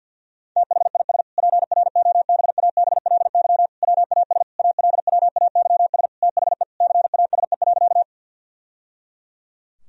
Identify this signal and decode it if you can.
Morse code, heard as a keyed tone, about 50 wpm, 700 Hz. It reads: THIS PROBABLY WAS ALWAYS THE CASE.
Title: Sentences from Top 1000 Words - Review Part 4 - 50wpm